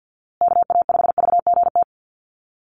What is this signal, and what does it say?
Morse code, heard as a keyed tone, about 50 wpm, 700 Hz. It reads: KA5VGT
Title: USA Call Signs Repeated 3x 50wpm